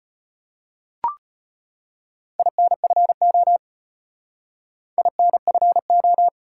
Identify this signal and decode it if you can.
Morse code, heard as a keyed tone, about 32 wpm, 700 Hz. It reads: INFO INFO